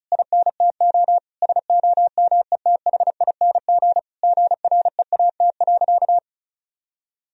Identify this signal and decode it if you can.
Morse code, heard as a keyed tone, about 36 wpm, 700 Hz. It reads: INTO SOMETHING GREAT.
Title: Sentences from Top 1000 Words - Review Part 2 - 35wpm